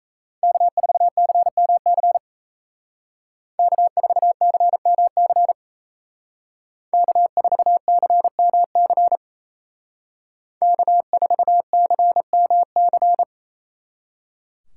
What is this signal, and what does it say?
Morse code, heard as a keyed tone, about 35 wpm, 700 Hz. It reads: K4CMC K4CMC K4CMC K4CMC